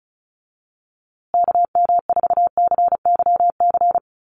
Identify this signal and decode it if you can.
Morse code, heard as a keyed tone, about 34 wpm, 700 Hz. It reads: KM4CYC
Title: USA Call Signs Repeated 3x 35wpm